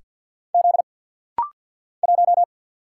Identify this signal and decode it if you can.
Morse code, heard as a keyed tone, about 49 wpm, 700 Hz. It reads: Z 1
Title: Single Letter-Number - Rapid-Fire 50wpm